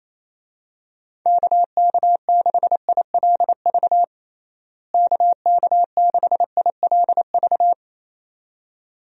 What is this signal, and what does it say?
Morse code, heard as a keyed tone, about 28 wpm, 700 Hz. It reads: KK6ILV KK6ILV
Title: USA Call Signs - 6 Characters - Repeated 2x - 28wpm